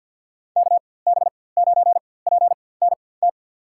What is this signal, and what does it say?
Morse code, heard as a keyed tone, about 53 wpm, spaced as for 24 wpm, 700 Hz. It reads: KB9PNT
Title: USA Call Signs - 6 Characters - 4x Character Spacing - 50wpm